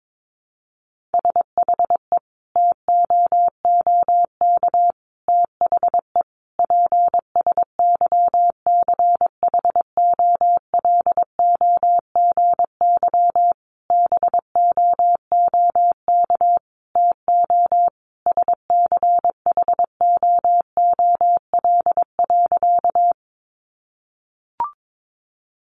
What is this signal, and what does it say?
Morse code, heard as a keyed tone, about 22 wpm, 700 Hz. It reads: SHE TOOK THE PSYCHOLOGY BOOK TO SCHOOL.
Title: Sentences from Top 1400 Words 22wpm